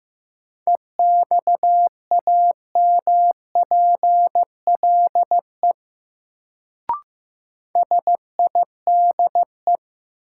Morse code, heard as a keyed tone, about 15 wpm, 700 Hz. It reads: EXAMPLE SIDE
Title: Sets of 2 Words - New Word Drill - 15wpm